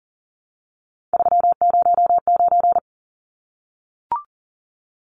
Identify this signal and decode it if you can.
Morse code, heard as a keyed tone, about 40 wpm, 700 Hz. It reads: 309
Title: Numbers - 3-Digit 40wpm